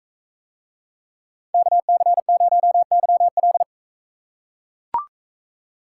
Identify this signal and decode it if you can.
Morse code, heard as a keyed tone, about 41 wpm, 700 Hz. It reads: KC0YL